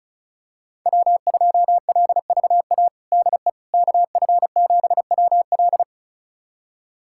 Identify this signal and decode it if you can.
Morse code, heard as a keyed tone, about 34 wpm, 700 Hz. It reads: W2LVA DE KF7WL
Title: USA Callsign DE Callsign - 5 Characters - 35wpm